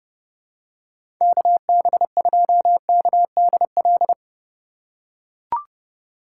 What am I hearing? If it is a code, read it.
Morse code, heard as a keyed tone, about 30 wpm, 700 Hz. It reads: KB2KDL
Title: USA Call Signs - 6 Characters - 2x Character Spacing - 30wpm